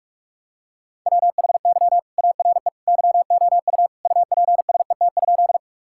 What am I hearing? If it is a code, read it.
Morse code, heard as a keyed tone, about 43 wpm, 700 Hz. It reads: WHY ARE YOU UPSET?
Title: Sentences from Top 1000 Words - Review Part 5 - 45wpm